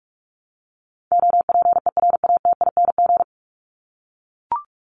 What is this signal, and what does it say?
Morse code, heard as a keyed tone, about 44 wpm, 700 Hz. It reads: OPERATING
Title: ING Suffix - Top 100 Words with Base Words 45wpm